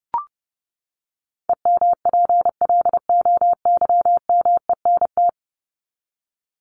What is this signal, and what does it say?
Morse code, heard as a keyed tone, about 30 wpm, 700 Hz. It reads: EMPLOYMENT
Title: Top 1251-1500 Words 30wpm